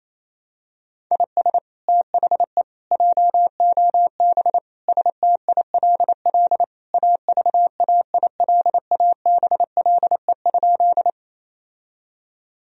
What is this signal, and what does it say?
Morse code, heard as a keyed tone, about 28 wpm, 700 Hz. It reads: IS THE JOB STILL AVAILABLE?
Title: Sentences from Top 1000 Words - Review Part 3 - 28wpm